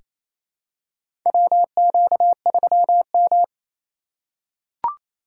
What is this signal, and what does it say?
Morse code, heard as a keyed tone, about 27 wpm, 700 Hz. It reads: WQ3M
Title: USA Call Signs - 4 Characters - 4x Character Spacing - 28wpm